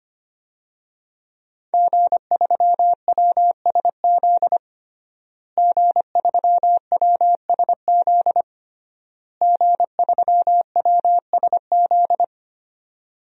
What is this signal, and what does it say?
Morse code, heard as a keyed tone, about 25 wpm, 700 Hz. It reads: G3WSZ G3WSZ G3WSZ
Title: DXCC-100 Call Signs - 5 Characters - Repeated 3x - 25wpm